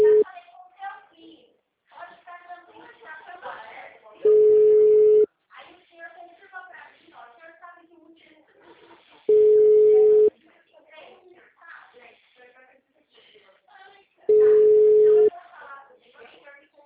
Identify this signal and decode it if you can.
Ringback tone (Europe, 425 Hz), with speech.
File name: out- -2026